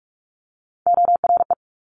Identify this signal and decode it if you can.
Morse code, heard as a keyed tone, about 46 wpm, 700 Hz. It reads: ORE
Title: Multi-letter Phonemes Quick Fire 45wpm